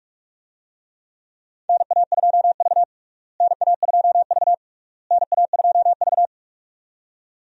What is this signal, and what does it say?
Morse code, heard as a keyed tone, about 46 wpm, 700 Hz. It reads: NA2V NA2V NA2V
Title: USA Call Signs - 4 Characters - Repeated 3x - 45wpm